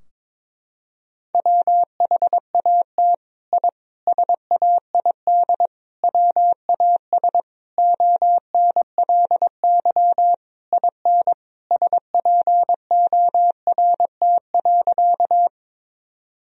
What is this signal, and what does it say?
Morse code, heard as a keyed tone, about 21 wpm, 700 Hz. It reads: WHAT I SAID WAS ONLY IN SPORT.
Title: Sentences from Top 1000 Words - Review Part 4 - 22wpm